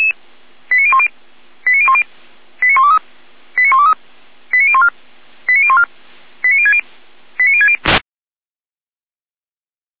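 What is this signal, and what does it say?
Five-tone selective call (ZVEI), heard as quick runs of five tones, tones of about 70 ms. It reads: 89010 89010 89123 89123 89014 89014 89060 89060
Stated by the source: ZVEI five-tone selective call 89163 received via SDR.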